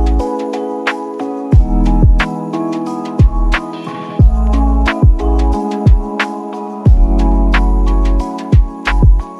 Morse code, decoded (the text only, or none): none